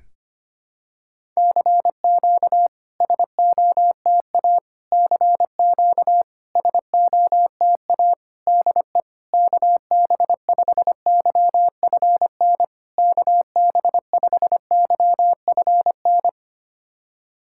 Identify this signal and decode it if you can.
Morse code, heard as a keyed tone, about 25 wpm, 700 Hz. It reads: CQ SOTA CQ SOTA DE KB5YFN KB5YFN
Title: SOTA QSO Practice 25wpm